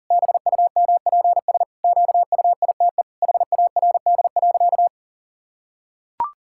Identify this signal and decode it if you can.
Morse code, heard as a keyed tone, about 40 wpm, 700 Hz. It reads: BUMPS QUITE HARD.
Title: Pangram Challenge v2 - 40wpm